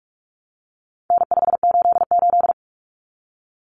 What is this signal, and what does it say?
Morse code, heard as a keyed tone, about 44 wpm, 700 Hz. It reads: N588
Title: Common Cut Numbers - 4-Digit 45wpm